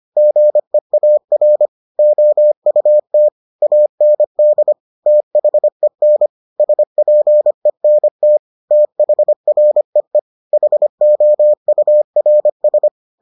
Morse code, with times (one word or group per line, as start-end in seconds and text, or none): 0.2-1.7 GEAR
2.0-3.3 OUT
3.6-4.7 AND
5.1-6.3 THEN
6.6-8.4 SPENT
8.7-10.2 THREE
10.5-12.9 HOURS